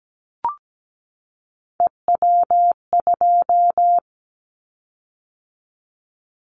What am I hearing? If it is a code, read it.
Morse code, heard as a keyed tone, about 17 wpm, 700 Hz. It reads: EW2
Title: Callsign Prefixes for Top 100 DXCC - 17wpm